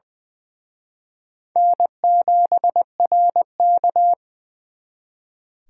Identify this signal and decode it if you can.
Morse code, heard as a keyed tone, about 24 wpm, 700 Hz. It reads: N7RK
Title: Speed Racing - USA Call Signs 20wpm